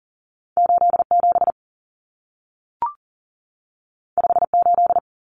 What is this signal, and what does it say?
Morse code, heard as a keyed tone, about 40 wpm, 700 Hz. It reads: 87 58
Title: Numbers - 2-Digit 40wpm